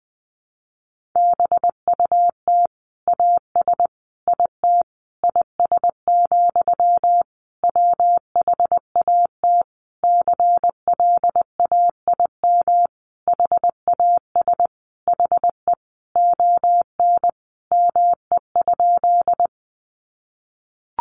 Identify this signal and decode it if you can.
Morse code, heard as a keyed tone, about 20 wpm, 700 Hz. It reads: BUT AS IT IS, WHAT CLAIM HAS HE ON ME?
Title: Sentences from Top 1500 Words 20wpm